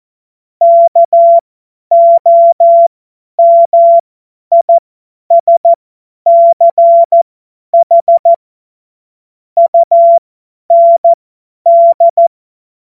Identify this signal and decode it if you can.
Morse code, heard as a keyed tone, about 14 wpm, spaced as for 10 wpm, 680 Hz. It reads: KOMISCH UND
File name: Morsekurs / Witze / Witz 09 Januar 2024 50 F70